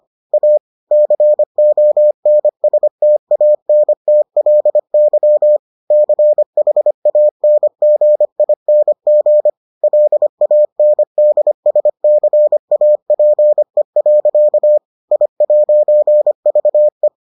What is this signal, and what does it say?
Morse code, heard as a keyed tone, about 25 wpm, 600 Hz. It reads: A CONSTANTLY CHANGING LANDSCAPE. I'VE